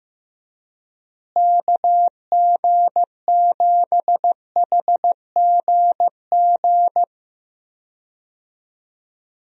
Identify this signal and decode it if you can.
Morse code, heard as a keyed tone, about 15 wpm, 700 Hz. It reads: KG7HGG
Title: USA Call Signs - 6 Characters - Repeated 3x - 15wpm